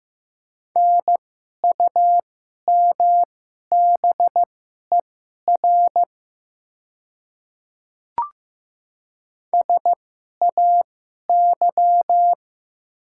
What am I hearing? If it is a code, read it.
Morse code, heard as a keyed tone, about 15 wpm, spaced as for 11 wpm, 700 Hz. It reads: NUMBER SAY
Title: 2x Character Spacing - Top 200 Words - 15wpm